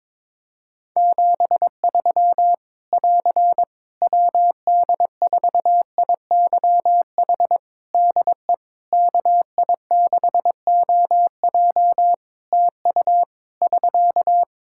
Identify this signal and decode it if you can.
Morse code, heard as a keyed tone, about 23 wpm, 700 Hz. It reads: 73 <AR> WD4IYH DE KI6OJ TU <SK>
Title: K1USN SST QSO Practice 22wpm